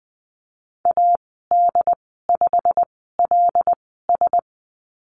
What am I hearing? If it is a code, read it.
Morse code, heard as a keyed tone, about 20 wpm, spaced as for 15 wpm, 700 Hz. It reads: AD5LS
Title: 128 Lesson 21g - Number 6 - Call Signs 20wpm